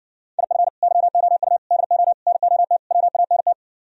Morse code, heard as a keyed tone, about 68 wpm, 700 Hz. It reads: IF YOU DO NOT WANT